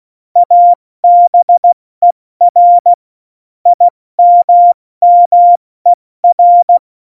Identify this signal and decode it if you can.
Morse code, heard as a keyed tone, about 16 wpm, spaced as for 14 wpm, 700 Hz. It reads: ABER IMMER
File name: Morsekurs / Witze / Witz 21 Juli 2020, 70 F80